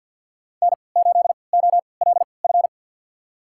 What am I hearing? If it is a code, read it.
Morse code, heard as a keyed tone, about 50 wpm, spaced as for 28 wpm, 700 Hz. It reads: N8OLF
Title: USA Call Signs - 5 Characters - 3x Character Spacing - 50wpm